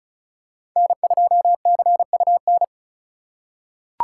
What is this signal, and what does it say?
Morse code, heard as a keyed tone, about 36 wpm, 700 Hz. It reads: N2CUN
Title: USA Call Signs - 5 Characters - 2x Character Spacing - 35wpm